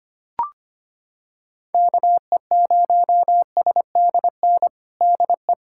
Morse code, heard as a keyed tone, about 25 wpm, 700 Hz. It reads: KE0SDN DE
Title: USA Callsign DE Callsign - 6 Characters - 25wpm